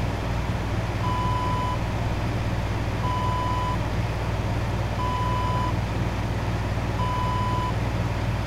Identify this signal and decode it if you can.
There is no code.